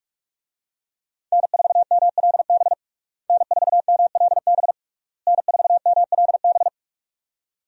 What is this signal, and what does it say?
Morse code, heard as a keyed tone, about 46 wpm, 700 Hz. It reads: N4MLB N4MLB N4MLB
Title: USA Call Signs - 5 Characters - Repeated 3x - 45wpm